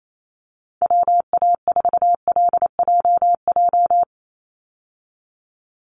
Morse code, heard as a keyed tone, about 28 wpm, 700 Hz. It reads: WA4LJJ
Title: Call Signs Repeated 3x - 28wpm